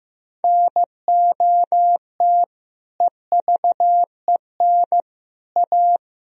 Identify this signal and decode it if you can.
Morse code, heard as a keyed tone, about 15 wpm, 700 Hz. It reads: NOT EVEN A
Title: Sentences from Top 1000 Words - Review Part 1 - 15wpm